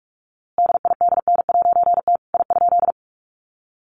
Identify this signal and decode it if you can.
Morse code, heard as a keyed tone, about 44 wpm, 700 Hz. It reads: DIDN'T I?
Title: A Sound Of Thunder 45wpm